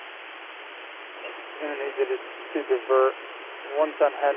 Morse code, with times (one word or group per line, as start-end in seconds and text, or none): none